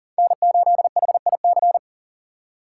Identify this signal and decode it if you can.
Morse code, heard as a keyed tone, about 38 wpm, 700 Hz. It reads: N8HIC
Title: USA Call Signs - 5 Characters - 6x Character Spacing - 40wpm